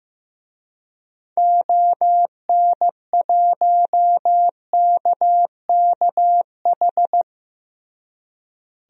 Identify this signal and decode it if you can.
Morse code, heard as a keyed tone, about 15 wpm, 700 Hz. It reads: ON1KKH
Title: DXCC-100 Call Signs - 6 Characters - Repeated 2x - 15wpm